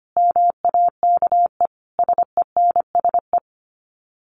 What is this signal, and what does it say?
Morse code, heard as a keyed tone, about 25 wpm, 700 Hz. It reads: MAKE SENSE
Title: Sets of 5 Words Encore2 25wpm